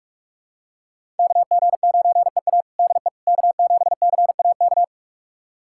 Morse code, heard as a keyed tone, about 42 wpm, 700 Hz. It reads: KG9EA DE K7CAK
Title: USA Callsign DE Callsign - 5 Characters - 45wpm